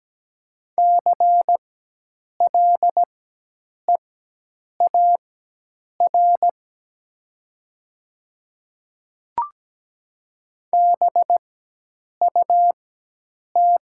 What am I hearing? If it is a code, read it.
Morse code, heard as a keyed tone, about 17 wpm, spaced as for 8 wpm, 700 Hz. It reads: CLEAR BUT
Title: 4x Character Spacing - Top 500 Words - 17wpm